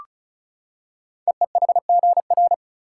Morse code, heard as a keyed tone, about 36 wpm, 690 Hz. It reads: EEHGR
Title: Code Groups - 5 Letters - 35wpm